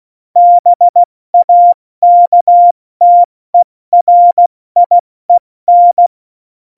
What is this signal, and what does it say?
Morse code, heard as a keyed tone, about 16 wpm, 700 Hz. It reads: BAKTERIEN
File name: Morsekurs / Witze / Witz 29. August 17, Tempo 70 F80